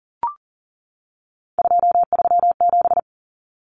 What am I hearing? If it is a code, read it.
Morse code, heard as a keyed tone, about 40 wpm, 700 Hz. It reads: 237
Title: Numbers - 3-Digit 40wpm